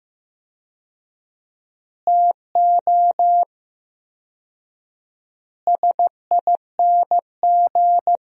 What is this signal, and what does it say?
Morse code, heard as a keyed tone, about 15 wpm, 700 Hz. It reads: TO SING
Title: 4x Word-Spacing - Sentences from Top 400 Words - 15wpm